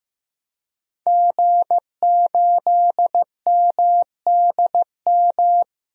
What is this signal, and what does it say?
Morse code, heard as a keyed tone, about 15 wpm, 700 Hz. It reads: G8MDM
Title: DXCC-100 Call Signs - 5 Characters - 15wpm